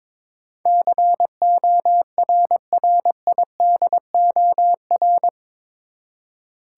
Morse code, heard as a keyed tone, about 22 wpm, 700 Hz. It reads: CORRIDOR
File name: practice / 22wpm / Eight Letter Words - 22wpm